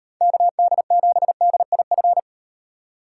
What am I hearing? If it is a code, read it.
Morse code, heard as a keyed tone, about 35 wpm, 700 Hz. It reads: KD7DIF